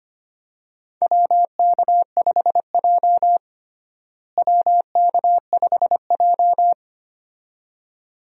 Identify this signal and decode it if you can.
Morse code, heard as a keyed tone, about 25 wpm, 700 Hz. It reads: WK5J WK5J
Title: USA Call Signs - 4 Characters - Repeated 2x - 25wpm